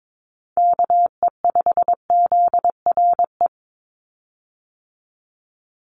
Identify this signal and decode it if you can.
Morse code, heard as a keyed tone, about 22 wpm, 700 Hz. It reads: KE5ZRE
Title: USA Call Signs 22wpm